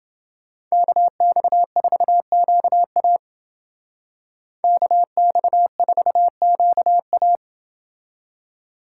Morse code, heard as a keyed tone, about 28 wpm, 700 Hz. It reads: KX4QA KX4QA